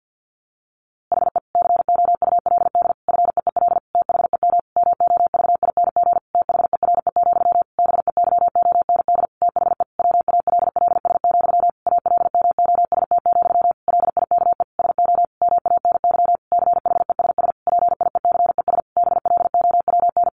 Morse code, heard as a keyed tone, about 50 wpm, 700 Hz. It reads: HE COULD FEEL THEM MOVING THERE, BEYOND THE WALLS, ALMOST, LIKE SO MANY CHESS PIECES BLOWN
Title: A Sound Of Thunder 50wpm